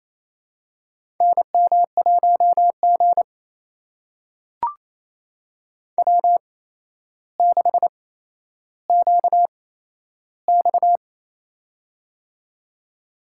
Morse code, heard as a keyed tone, about 28 wpm, spaced as for 12 wpm, 700 Hz. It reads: NM1G W6QX
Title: USA Call Signs - 4 Characters - 8x Character Spacing - 28wpm